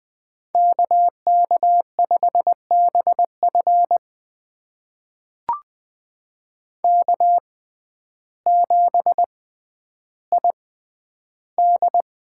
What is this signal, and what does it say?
Morse code, heard as a keyed tone, about 20 wpm, spaced as for 11 wpm, 700 Hz. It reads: KK5BF K7ID
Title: USA Call Signs - 5 Characters - 6x Character Spacing - 20wpm